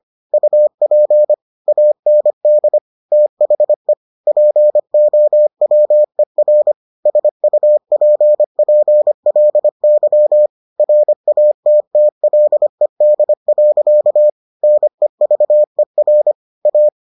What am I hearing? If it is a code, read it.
Morse code, heard as a keyed tone, about 25 wpm, 600 Hz. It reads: UP AND THE POWER SUPPLY RATTLED. NEVER A